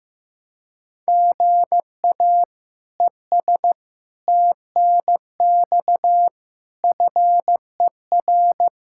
Morse code, heard as a keyed tone, about 15 wpm, 700 Hz. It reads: GA ES TNX FER
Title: Practice QSO - Exchange 1 - Beginning - 15wpm